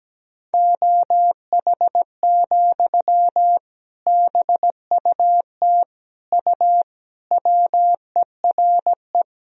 Morse code, heard as a keyed tone, about 17 wpm, 700 Hz. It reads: OH, BUT U WERE
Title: Sentences from Top 500 Words and Abbreviations 17wpm